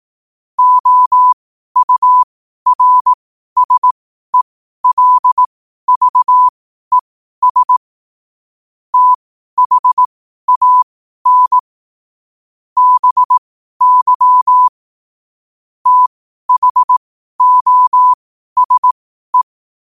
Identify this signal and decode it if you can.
Morse code, heard as a keyed tone, about 18 wpm, spaced as for 12 wpm, 1000 Hz. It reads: OURSELVES THAN BY THOSE
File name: qotdfiles / 12wpm